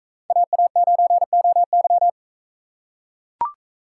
Morse code, heard as a keyed tone, about 41 wpm, 700 Hz. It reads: AA9OY